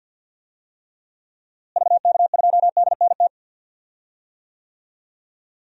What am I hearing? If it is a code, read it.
Morse code, heard as a keyed tone, about 52 wpm, 700 Hz. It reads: VK2DNT
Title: DXCC-100 Call Signs - 6 Characters - Repeated 3x - 50wpm